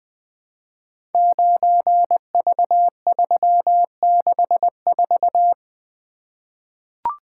Morse code, heard as a keyed tone, about 20 wpm, 700 Hz. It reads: 9V364
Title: Code Groups - 5 Letter:Numbers - 20wpm